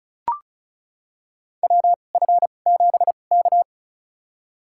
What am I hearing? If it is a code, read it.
Morse code, heard as a keyed tone, about 36 wpm, spaced as for 25 wpm, 700 Hz. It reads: WF7K
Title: USA Call Signs - 4 Characters - 2x Character Spacing - 35wpm